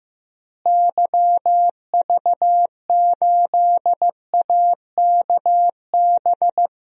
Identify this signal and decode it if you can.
Morse code, heard as a keyed tone, about 15 wpm, 700 Hz. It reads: YV8AKB
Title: DXCC-100 Call Signs - 6 Characters - Repeated 3x - 15wpm